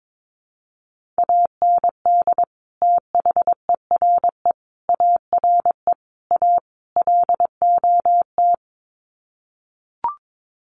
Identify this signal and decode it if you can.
Morse code, heard as a keyed tone, about 22 wpm, 700 Hz. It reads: AND THERE ARE A LOT
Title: Sets of 5 Words Encore2 22wpm